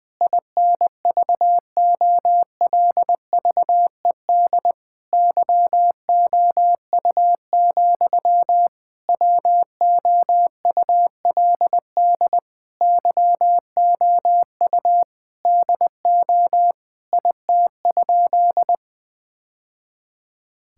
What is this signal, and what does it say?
Morse code, heard as a keyed tone, about 20 wpm, 700 Hz. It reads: INVOLVED YOU, WOULD YOU DO IT?
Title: Sentences from Top 1000 Words - Review Part 5 - 20wpm